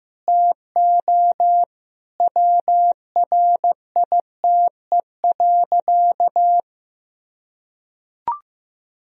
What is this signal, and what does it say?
Morse code, heard as a keyed tone, about 15 wpm, 700 Hz. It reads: TO WRITE.
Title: Sentences from Top 1000 Words - Review Part 1 - 15wpm